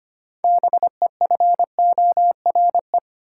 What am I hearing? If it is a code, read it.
Morse code, heard as a keyed tone, about 25 wpm, 700 Hz. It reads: BEFORE
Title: Six Letter Words - 25wpm